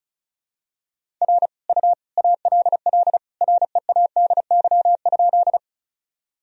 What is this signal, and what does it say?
Morse code, heard as a keyed tone, about 36 wpm, 700 Hz. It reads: R U ALL READY?
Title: Sentences from Top 400 Words and Abbreviations 35wpm